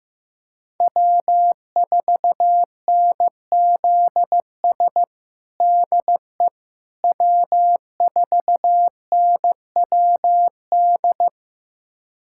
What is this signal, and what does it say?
Morse code, heard as a keyed tone, about 15 wpm, 700 Hz. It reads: W4NZS DE W4NWD
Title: USA Callsign DE Callsign - 5 Characters - 15wpm